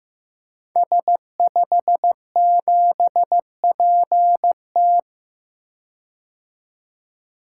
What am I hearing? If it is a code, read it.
Morse code, heard as a keyed tone, about 15 wpm, 700 Hz. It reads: S57PT
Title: Code Groups - 5 Letter:Numbers - 15wpm